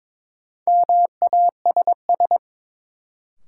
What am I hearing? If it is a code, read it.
Morse code, heard as a keyed tone, about 22 wpm, 700 Hz. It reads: MASS
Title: Sets of 5 Words Encore - New Word Drill - 22wpm